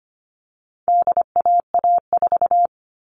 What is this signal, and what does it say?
Morse code, heard as a keyed tone, about 25 wpm, 700 Hz. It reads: DAA4
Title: All Cut Numbers - 4-Digit 25wpm